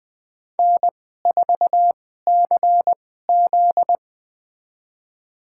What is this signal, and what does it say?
Morse code, heard as a keyed tone, about 20 wpm, spaced as for 14 wpm, 700 Hz. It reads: N4CZ
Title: USA Call Signs - 4 Characters - 2x Character Spacing - 20wpm